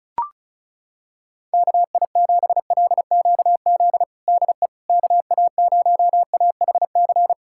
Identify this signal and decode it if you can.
Morse code, heard as a keyed tone, about 36 wpm, 700 Hz. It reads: KI7LQZ DE KA0AHC
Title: USA Callsign DE Callsign 35wpm